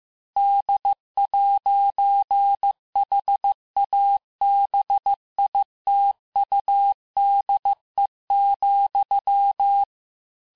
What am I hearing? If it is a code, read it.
Morse code, heard as a keyed tone, about 15 wpm, 800 Hz. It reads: D'HABITUDE,
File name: cours-du-lundi files / cours-1585-texte-FIN-de-QSO-vitesse-14mpm